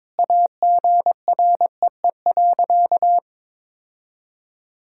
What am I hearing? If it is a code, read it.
Morse code, heard as a keyed tone, about 22 wpm, 700 Hz. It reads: AGREE.
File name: practice / 22wpm / Sentences with 4-Word Phrases - 22wpm